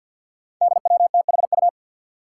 Morse code, heard as a keyed tone, about 47 wpm, 700 Hz. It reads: DWTHU
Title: Five Challenging Letter-Number Combinations 50wpm